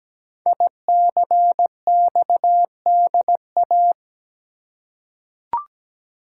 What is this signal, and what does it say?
Morse code, heard as a keyed tone, about 17 wpm, 700 Hz. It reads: ICXDA
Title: Code Groups - 5 Letter:Numbers - 17wpm